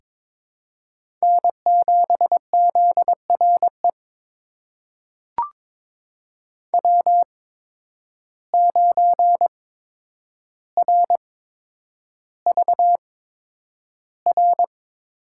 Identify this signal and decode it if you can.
Morse code, heard as a keyed tone, about 22 wpm, spaced as for 9 wpm, 700 Hz. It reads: N7ZRE W9RVR
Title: USA Call Signs - 5 Characters - 8x Character Spacing - 22wpm